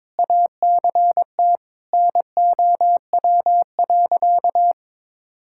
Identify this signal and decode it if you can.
Morse code, heard as a keyed tone, about 22 wpm, 700 Hz. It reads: ACT NOW.
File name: Sentences with 9-Word Phrases - 22wpm